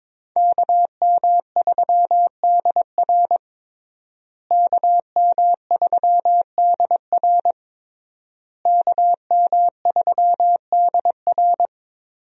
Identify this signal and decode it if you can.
Morse code, heard as a keyed tone, about 22 wpm, 700 Hz. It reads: KM3DR KM3DR KM3DR
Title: USA Call Signs - 5 Characters - Repeated 3x - 22wpm